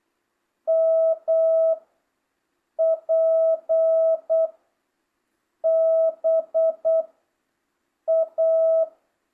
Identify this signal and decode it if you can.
Morse code, heard as a keyed tone, about 8 wpm, 650 Hz. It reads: M P B A